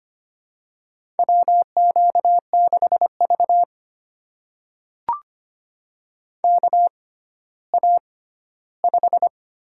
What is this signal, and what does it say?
Morse code, heard as a keyed tone, about 25 wpm, spaced as for 14 wpm, 700 Hz. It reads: WQ6V KA5
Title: USA Call Signs - 4 Characters - 6x Character Spacing - 25wpm